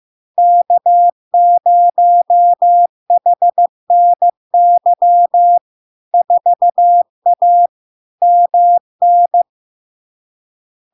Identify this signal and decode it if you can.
Morse code, heard as a keyed tone, about 15 wpm, 700 Hz. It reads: K0HNY 4A MN
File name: LARK-FD-2022-Log-in-CW-15wpm-700-Hz